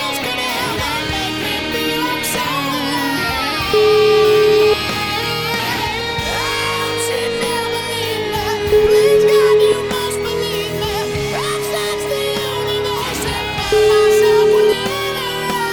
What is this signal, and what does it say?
Ringback tone (Europe, 425 Hz), with noise.